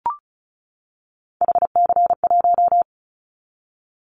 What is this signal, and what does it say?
Morse code, heard as a keyed tone, about 34 wpm, 700 Hz. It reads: HC1
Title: Callsign Prefixes for Top 100 DXCC - 35wpm